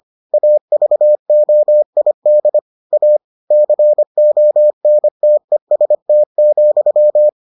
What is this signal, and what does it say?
Morse code, heard as a keyed tone, about 25 wpm, 600 Hz. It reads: AVOID A CONTEST,